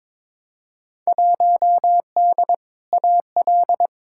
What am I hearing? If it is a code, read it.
Morse code, heard as a keyed tone, about 22 wpm, 700 Hz. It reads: 1D AL
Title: Field Day Exchanges - 22wpm